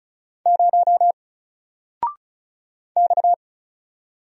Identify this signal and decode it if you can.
Morse code, heard as a keyed tone, about 33 wpm, 700 Hz. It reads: 0 X
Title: Single Letter-Number Alt 35wpm